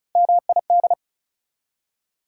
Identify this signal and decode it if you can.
Morse code, heard as a keyed tone, about 36 wpm, 700 Hz. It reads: MID
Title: Prefixes and Suffixes 35wpm